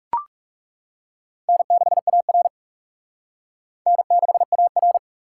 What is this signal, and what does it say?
Morse code, heard as a keyed tone, about 38 wpm, 700 Hz. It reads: N6AR N6AR